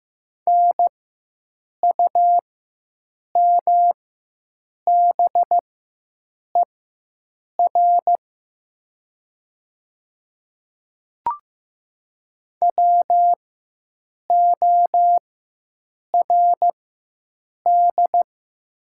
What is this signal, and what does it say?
Morse code, heard as a keyed tone, about 15 wpm, spaced as for 7 wpm, 700 Hz. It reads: NUMBER WORD
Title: 4x Character Spacing - Top 100 Words - 15wpm